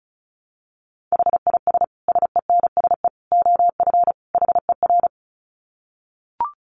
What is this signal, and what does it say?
Morse code, heard as a keyed tone, about 34 wpm, 700 Hz. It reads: HIS SENSE OF HER
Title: Dit Challenge 35wpm